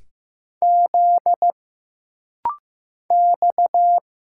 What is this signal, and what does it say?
Morse code, heard as a keyed tone, about 15 wpm, 700 Hz. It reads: Z X
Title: Single Letters - Rapid-Fire 15wpm